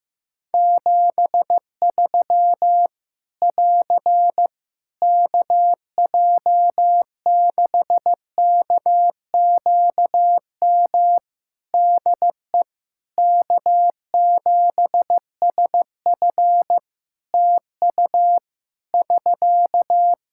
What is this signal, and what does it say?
Morse code, heard as a keyed tone, about 15 wpm, 700 Hz. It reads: 73 <AR> KJ6KQM DE K7SF TU <SK>
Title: Practice QSO - Exchange 4 - Ending - 15wpm